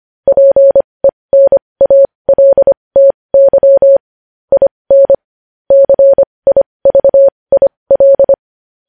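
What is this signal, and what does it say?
Morse code, heard as a keyed tone, about 25 wpm, spaced as for 20 wpm, 550 Hz. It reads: PENALTY IN CIVIL
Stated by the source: Morse code news headlines practice at 20 wpm.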